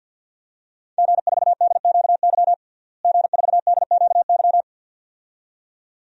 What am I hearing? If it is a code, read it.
Morse code, heard as a keyed tone, about 51 wpm, 700 Hz. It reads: G4DQY G4DQY
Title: DXCC-100 Call Signs - 5 Characters - Repeated 2x - 50wpm